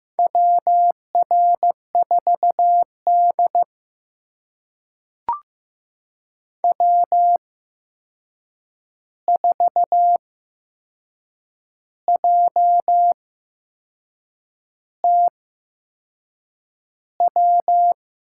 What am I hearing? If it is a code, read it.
Morse code, heard as a keyed tone, about 15 wpm, spaced as for 6 wpm, 700 Hz. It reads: WR4D W4JT W
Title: USA Call Signs - 8x Character Spacing - 15wpm